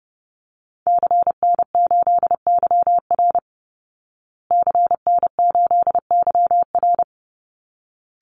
Morse code, heard as a keyed tone, about 30 wpm, 700 Hz. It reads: CN8YR CN8YR
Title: Call Signs Repeated 3x - 30wpm